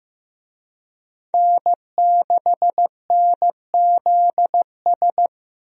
Morse code, heard as a keyed tone, about 16 wpm, 700 Hz. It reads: N6NZS
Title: USA Call Signs - 5 Characters - 15wpm